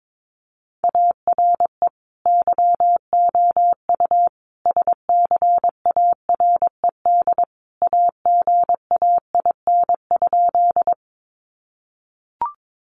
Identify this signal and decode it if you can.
Morse code, heard as a keyed tone, about 22 wpm, 700 Hz. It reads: ARE YOU SCARED AGAIN?
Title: Sentences from Top 700 Words 22wpm